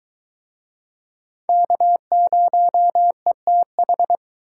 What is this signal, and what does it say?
Morse code, heard as a keyed tone, about 21 wpm, 700 Hz. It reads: K0ETH